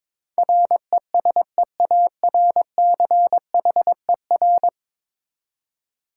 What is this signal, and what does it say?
Morse code, heard as a keyed tone, about 22 wpm, 700 Hz. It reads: RESEARCHER
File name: practice / 22wpm / 10 Letter Words - First 3 Letters Given - 22wpm